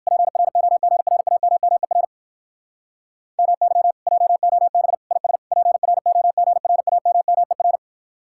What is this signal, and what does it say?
Morse code, heard as a keyed tone, about 55 wpm, 700 Hz. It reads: PROGRAMMER MY JOB IS PROGRAMMER
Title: Speed-Racing Common QSO Phrases - 45wpm